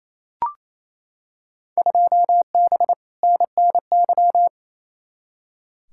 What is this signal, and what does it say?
Morse code, heard as a keyed tone, about 28 wpm, 700 Hz. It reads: 2B NNY
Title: Field Day Exchanges Encore - 28wpm